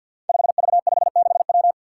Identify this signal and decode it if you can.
Morse code, heard as a keyed tone, about 53 wpm, 700 Hz. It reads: 5V56W
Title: Five Challenging Letter-Number Combinations 50wpm